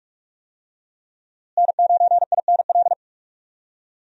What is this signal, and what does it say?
Morse code, heard as a keyed tone, about 44 wpm, 700 Hz. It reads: N9INL
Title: USA Call Signs - 5 Characters - Repeated 3x - 45wpm